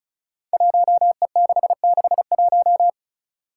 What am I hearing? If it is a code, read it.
Morse code, heard as a keyed tone, about 36 wpm, 700 Hz. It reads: 1E661
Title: Five Challenging Letter-Number Combinations 35wpm